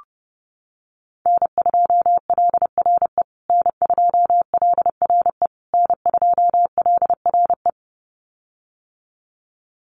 Morse code, heard as a keyed tone, about 30 wpm, 700 Hz. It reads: N2LRE N2LRE N2LRE
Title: USA Call Signs Repeated 3x 30wpm